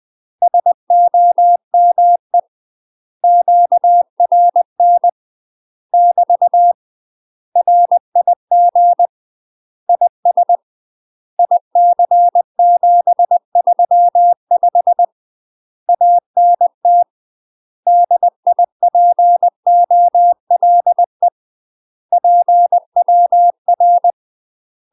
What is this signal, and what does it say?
Morse code, heard as a keyed tone, about 20 wpm, 700 Hz. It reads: SOME QRN <BT> RIG IS IC735 ANT DIPOLE PWR
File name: UK MASTER QSOs TRAINING / 20WPM / UK MASTER QSOs 0040